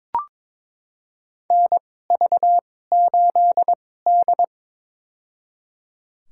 Morse code, heard as a keyed tone, about 22 wpm, spaced as for 16 wpm, 700 Hz. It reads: NV8D
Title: USA Call Signs - 2x Character Spacing - 22wpm